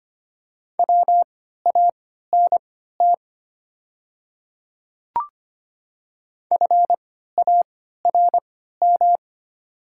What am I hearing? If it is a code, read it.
Morse code, heard as a keyed tone, about 25 wpm, spaced as for 14 wpm, 700 Hz. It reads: WANT FARM
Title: 3x Character Spacing - Top 500 Words - 25wpm